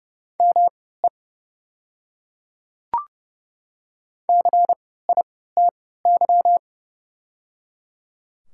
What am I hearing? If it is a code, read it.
Morse code, heard as a keyed tone, about 30 wpm, spaced as for 17 wpm, 700 Hz. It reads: ME CITY
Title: 3x Character Spacing - Top 300 Words - 30wpm